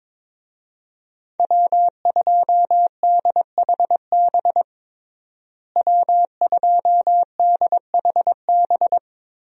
Morse code, heard as a keyed tone, about 21 wpm, 700 Hz. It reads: W2DHB W2DHB
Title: USA Call Signs - 5 Characters - Repeated 3x - 22wpm